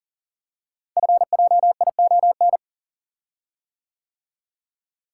Morse code, heard as a keyed tone, about 40 wpm, 700 Hz. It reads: FJION
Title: Code Groups - 5 Letters - 40wpm